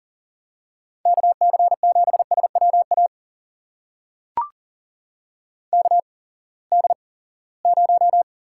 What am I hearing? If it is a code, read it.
Morse code, heard as a keyed tone, about 40 wpm, spaced as for 23 wpm, 700 Hz. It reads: KC7SWA KD0
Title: USA Call Signs - 6 Characters - 8x Character Spacing - 40wpm